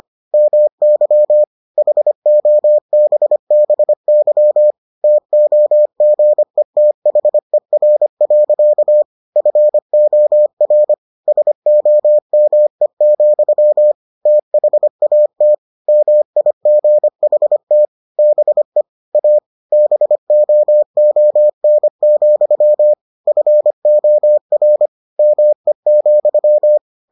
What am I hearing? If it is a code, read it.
Morse code, heard as a keyed tone, about 25 wpm, 600 Hz. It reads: MY HOBBY TOGETHER. FOR SOME, THAT MIGHT BE A BOON, FOR ME,